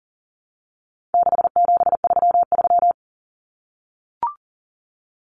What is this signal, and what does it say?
Morse code, heard as a keyed tone, about 40 wpm, 700 Hz. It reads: 6733
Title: Numbers - 4-Digit 40wpm